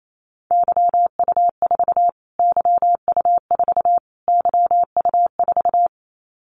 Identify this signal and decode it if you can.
Morse code, heard as a keyed tone, about 28 wpm, 700 Hz. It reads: YU4 YU4 YU4
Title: Callsign Prefixes for Top 25 DXCC - Repeated 3x - 28wpm